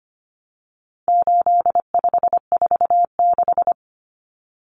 Morse code, heard as a keyed tone, about 25 wpm, 700 Hz. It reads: 8546
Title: Common Cut Numbers - 4-Digit 25wpm